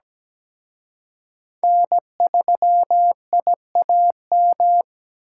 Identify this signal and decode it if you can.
Morse code, heard as a keyed tone, about 20 wpm, 700 Hz. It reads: N3IAM
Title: Speed Racing - USA 1x3 Call Signs 17wpm